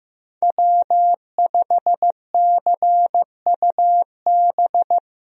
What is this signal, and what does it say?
Morse code, heard as a keyed tone, about 15 wpm, 700 Hz. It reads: W5CUB
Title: USA Call Signs - 5 Characters - 15wpm